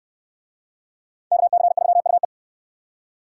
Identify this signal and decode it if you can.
Morse code, heard as a keyed tone, about 59 wpm, 700 Hz. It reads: KC3UE